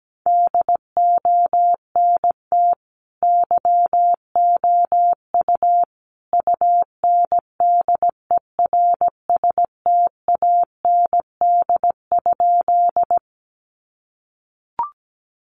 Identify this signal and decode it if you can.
Morse code, heard as a keyed tone, about 17 wpm, 700 Hz. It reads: DONT YOU UNDERSTAND?
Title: Sentences from Top 600 Words 17wpm